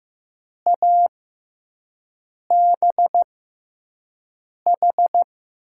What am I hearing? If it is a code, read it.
Morse code, heard as a keyed tone, about 16 wpm, spaced as for 5 wpm, 700 Hz. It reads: ABH
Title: USA Call Signs - 6 Characters - 6x Character Spacing - 15wpm